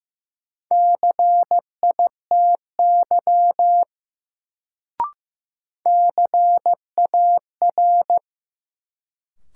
Morse code, heard as a keyed tone, about 15 wpm, 700 Hz. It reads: CITY CAR
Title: Sentences from Top 300 Words - New Word Drill - 15wpm